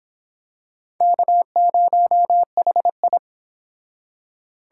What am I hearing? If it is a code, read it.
Morse code, heard as a keyed tone, about 25 wpm, 700 Hz. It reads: K0HI